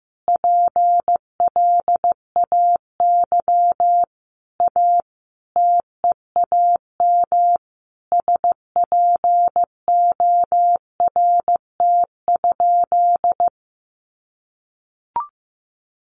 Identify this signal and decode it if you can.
Morse code, heard as a keyed tone, about 15 wpm, 700 Hz. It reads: PLAY A TEAM SPORT?
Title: Sentences from Top 1500 Words 15wpm